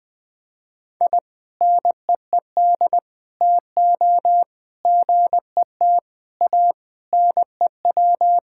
Morse code, heard as a keyed tone, about 20 wpm, 700 Hz. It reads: I NEED TO GET A NEW
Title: Sentences with 3-Word Phrases - 20wpm